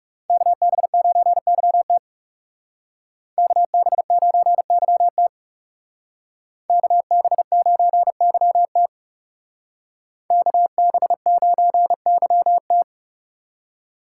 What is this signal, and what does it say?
Morse code, heard as a keyed tone, about 36 wpm, 700 Hz. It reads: KB9YT KB9YT KB9YT KB9YT
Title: Speed Racing - USA 2x2 Call Signs 30wpm